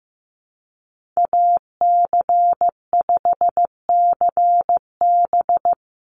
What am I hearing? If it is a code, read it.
Morse code, heard as a keyed tone, about 15 wpm, 700 Hz. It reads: AC5CB
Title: USA Call Signs 15wpm